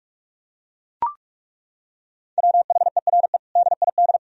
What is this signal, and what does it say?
Morse code, heard as a keyed tone, about 41 wpm, 700 Hz. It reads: WHERE DID
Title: Sentences from Top 1000 Words - Review Part 3 - 45wpm